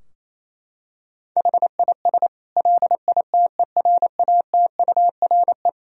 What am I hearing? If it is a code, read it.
Morse code, heard as a keyed tone, about 28 wpm, 700 Hz. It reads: HIS LITERATURE
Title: Sentences from Top 1000 Words - Review Part 4 - 28wpm